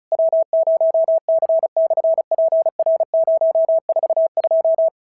Morse code, heard as a keyed tone, about 36 wpm, 650 Hz. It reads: W0C/PR042
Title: SOTA QSO Practice 35wpm